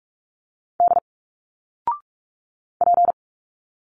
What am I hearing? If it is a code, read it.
Morse code, heard as a keyed tone, about 46 wpm, 700 Hz. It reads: D P
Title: Single Letters 45wpm